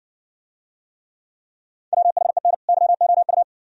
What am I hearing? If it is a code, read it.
Morse code, heard as a keyed tone, about 60 wpm, 700 Hz. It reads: WHEN YOU